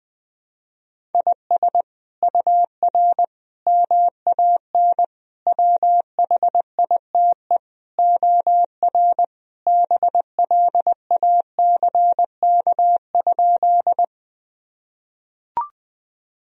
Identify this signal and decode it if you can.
Morse code, heard as a keyed tone, about 20 wpm, 700 Hz. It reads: IS UR MAN WHITE OR BLACK?
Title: Sentences from Top 400 Words and Abbreviations 20wpm